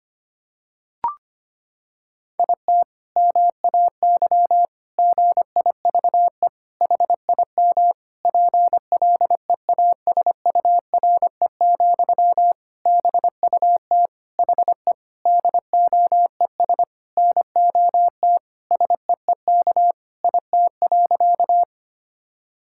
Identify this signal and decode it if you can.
Morse code, heard as a keyed tone, about 25 wpm, 700 Hz. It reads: IT MAY GIVE HIM PLEASURE, BUT HE DOES NOT SEEK IT.
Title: Sentences from Top 1600 Words 25wpm